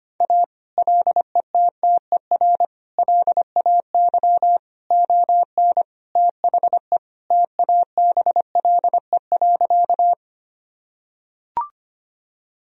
Morse code, heard as a keyed tone, about 25 wpm, 700 Hz. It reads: A LETTER LAY ON THE TABLE.
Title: Sentences from Top 1000 Words - Review Part 2 - 25wpm